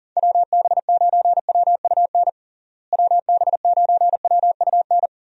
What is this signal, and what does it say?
Morse code, heard as a keyed tone, about 40 wpm, 700 Hz. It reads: WB9WUN WB9WUN
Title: USA Call Signs - 6 Characters - Repeated 2x - 40wpm